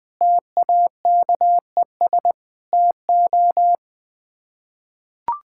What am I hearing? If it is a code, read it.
Morse code, heard as a keyed tone, about 20 wpm, 700 Hz. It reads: TAKES TO
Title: Sets of 6 Words - 20wpm